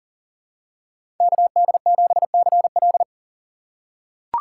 Morse code, heard as a keyed tone, about 40 wpm, 700 Hz. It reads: KD7CL
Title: USA Call Signs - 6x Character Spacing - 40wpm